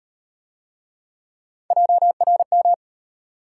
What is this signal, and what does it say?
Morse code, heard as a keyed tone, about 38 wpm, 700 Hz. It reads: JRM